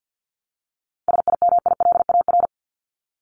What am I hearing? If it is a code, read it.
Morse code, heard as a keyed tone, about 50 wpm, 700 Hz. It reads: SIMILAR
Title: LY Suffix - Top 100 Words with Base Words 50wpm